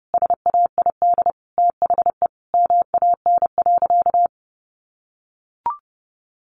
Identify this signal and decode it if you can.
Morse code, heard as a keyed tone, about 30 wpm, 700 Hz. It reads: SAID THE MAN.
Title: Sentences from Top 200 Words 30wpm